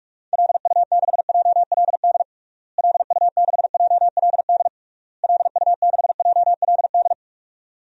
Morse code, heard as a keyed tone, about 46 wpm, 700 Hz. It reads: LU6JLD LU6JLD LU6JLD
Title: DXCC-100 Call Signs - 6 Characters - Repeated 3x - 45wpm